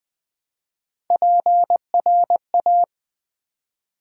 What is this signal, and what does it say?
Morse code, heard as a keyed tone, about 20 wpm, 700 Hz. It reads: PRA
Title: Build A Word - 1 Letter At A Time - 4-Letter Words - 20wpm